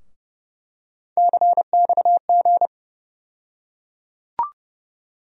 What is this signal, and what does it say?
Morse code, heard as a keyed tone, about 30 wpm, 700 Hz. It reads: CXG
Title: USA Call Sign Postfixes - 3 Letters 30wpm